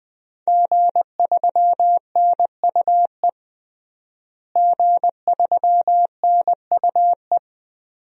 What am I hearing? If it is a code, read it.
Morse code, heard as a keyed tone, about 21 wpm, 700 Hz. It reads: G3NUE G3NUE
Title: DXCC-100 Call Signs - 5 Characters - Repeated 3x - 20wpm